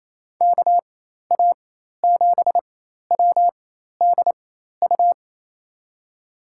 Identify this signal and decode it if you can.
Morse code, heard as a keyed tone, about 27 wpm, spaced as for 13 wpm, 700 Hz. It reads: KA7WDU
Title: USA Call Signs - 6 Characters - 4x Character Spacing - 28wpm